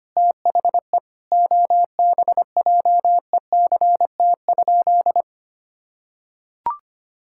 Morse code, heard as a keyed tone, about 25 wpm, 700 Hz. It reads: THE OBJECT?
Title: Sentences from Top 1000 Words - Review Part 5 - 25wpm